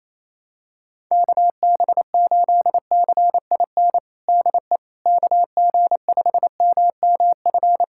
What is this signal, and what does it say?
Morse code, heard as a keyed tone, about 28 wpm, 700 Hz. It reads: KB8CIN DE KG5MMF
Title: USA Callsign DE Callsign - 6 Characters - 28wpm